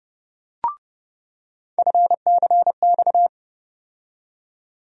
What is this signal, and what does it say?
Morse code, heard as a keyed tone, about 30 wpm, 700 Hz. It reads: FCX
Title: USA Call Sign Postfixes - 3 Letters 30wpm